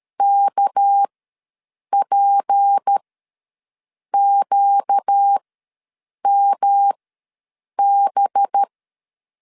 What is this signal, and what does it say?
Morse code, heard as a keyed tone, about 13 wpm, spaced as for 7 wpm, 800 Hz. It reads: KPQMB
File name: LEZIONI / 6 PKX0 / 6 PK ws 12wpm 3116